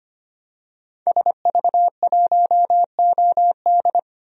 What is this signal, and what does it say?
Morse code, heard as a keyed tone, about 25 wpm, 700 Hz. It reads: SV1OD
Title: DXCC-100 Call Signs - 5 Characters - 25wpm